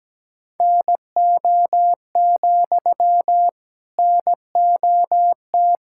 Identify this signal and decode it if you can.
Morse code, heard as a keyed tone, about 17 wpm, 700 Hz. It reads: NO, NOT
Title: Sentences from Top 1000 Words - Review Part 6 - 17wpm